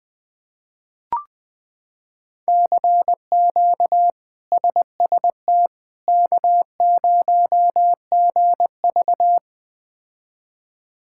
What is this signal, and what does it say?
Morse code, heard as a keyed tone, about 20 wpm, 700 Hz. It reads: CQ SST K0GV
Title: K1USN SST QSO Practice - 20wpm